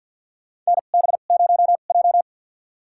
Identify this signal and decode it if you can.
Morse code, heard as a keyed tone, about 46 wpm, spaced as for 35 wpm, 700 Hz. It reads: NB0J
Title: USA Call Signs - 4 Characters - 2x Character Spacing - 50wpm